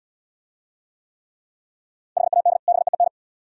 Morse code, heard as a keyed tone, about 71 wpm, 690 Hz. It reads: HAD BEEN